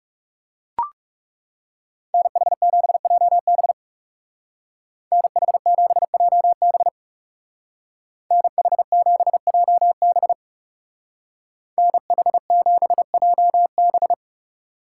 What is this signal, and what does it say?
Morse code, heard as a keyed tone, about 35 wpm, 700 Hz. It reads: NH7JB NH7JB NH7JB NH7JB
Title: Speed Racing - USA 2x2 Call Signs 30wpm